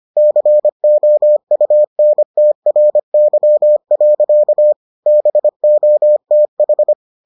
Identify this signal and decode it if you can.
Morse code, heard as a keyed tone, about 25 wpm, 600 Hz. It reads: COUNTRY. BOTH